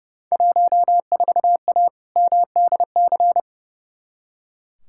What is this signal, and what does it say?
Morse code, heard as a keyed tone, about 30 wpm, 700 Hz. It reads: 14A MDC
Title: Field Day Exchanges Encore - 30wpm